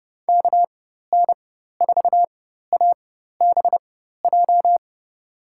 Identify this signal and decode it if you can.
Morse code, heard as a keyed tone, about 30 wpm, spaced as for 14 wpm, 700 Hz. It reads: KN4ABJ
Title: USA Call Signs - 6 Characters - 4x Character Spacing - 30wpm